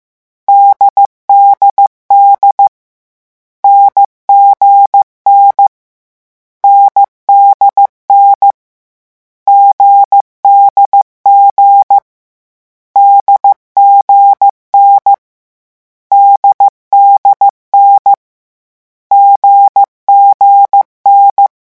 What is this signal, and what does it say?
Morse code, heard as a keyed tone, about 15 wpm, 800 Hz. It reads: DDD NGN NDN GDG DGN DDN GGN
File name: morse mp3 / lecon4-vitesse 14